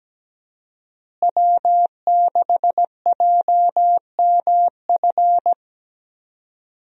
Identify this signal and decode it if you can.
Morse code, heard as a keyed tone, about 17 wpm, 700 Hz. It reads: W6JMF
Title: USA Call Signs - 5 Characters - Repeated 3x - 17wpm